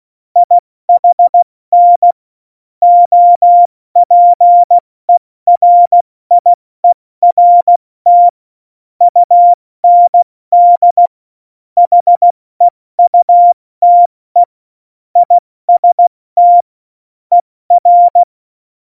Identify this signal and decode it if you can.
Morse code, heard as a keyed tone, about 16 wpm, 700 Hz. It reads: IHN OPERIERT UND HEUTE IST ER
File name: Morsekurs / Witze / Witz 01. Dezember 2020, 70 F80